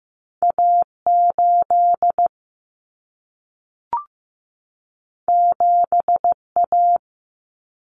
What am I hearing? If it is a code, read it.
Morse code, heard as a keyed tone, about 15 wpm, 700 Hz. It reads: A8 7A
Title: Common Cut Numbers - 2-Digit 15wpm